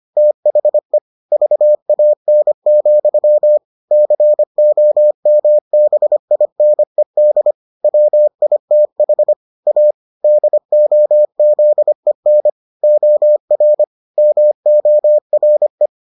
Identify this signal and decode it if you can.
Morse code, heard as a keyed tone, about 25 wpm, 600 Hz. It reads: THE VAN, COMBINED WITH A DOZEN OR MORE